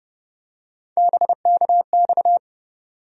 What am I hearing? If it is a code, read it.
Morse code, heard as a keyed tone, about 30 wpm, 700 Hz. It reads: BKX
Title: USA Call Sign Postfixes - 3 Letters 30wpm